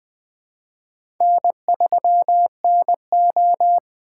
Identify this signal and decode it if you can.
Morse code, heard as a keyed tone, about 20 wpm, 700 Hz. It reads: N3NO
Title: USA Call Signs - 4 Characters - 3x Character Spacing - 20wpm